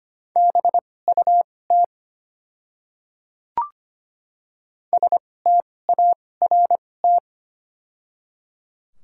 Morse code, heard as a keyed tone, about 25 wpm, spaced as for 18 wpm, 700 Hz. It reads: BUT START
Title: 2x Character Spacing - Top 500 Words - 25wpm